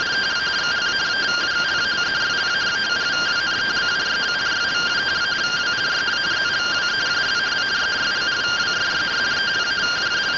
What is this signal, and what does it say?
Two-tone teleprinter signal, about 170 Hz shift.